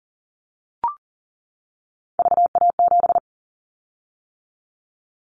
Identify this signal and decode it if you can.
Morse code, heard as a keyed tone, about 40 wpm, 700 Hz. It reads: VA7
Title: Callsign Prefixes for Top 100 DXCC - 40wpm